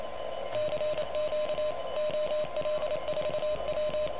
Morse code, heard as a keyed tone, about 28 wpm, 600 Hz, with noise. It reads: CQ OL4W
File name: truSDX